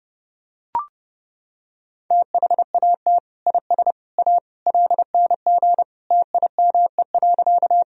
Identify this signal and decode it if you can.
Morse code, heard as a keyed tone, about 30 wpm, 700 Hz. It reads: THAT IS A LNG TIME.
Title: Sentences from Top 100 Words and Abbreviations 30wpm